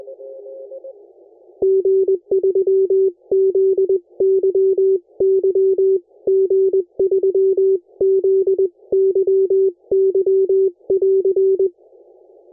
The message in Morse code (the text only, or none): G3ZYY G3ZYY <AR>